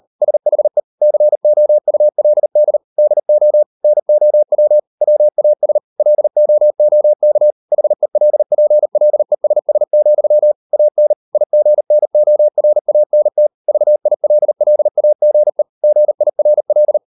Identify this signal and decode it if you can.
Morse code, heard as a keyed tone, about 38 wpm, 600 Hz. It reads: SHE COULD DO NOW WAS LOOK HELPLESS, AN IGNORANT VILLAGE GIRL